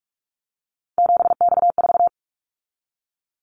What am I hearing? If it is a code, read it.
Morse code, heard as a keyed tone, about 46 wpm, 700 Hz. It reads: 7X4
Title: Callsign Prefixes for Top 100 DXCC - 45wpm